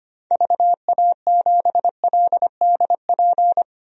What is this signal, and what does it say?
Morse code, heard as a keyed tone, about 25 wpm, 700 Hz. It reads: VA7LDP
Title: DXCC-100 Call Signs - 6 Characters - 25wpm